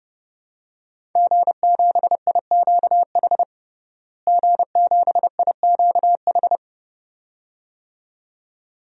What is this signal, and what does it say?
Morse code, heard as a keyed tone, about 32 wpm, 700 Hz. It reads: G7IQH G7IQH
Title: DXCC-100 Call Signs - 5 Characters - Repeated 2x - 30wpm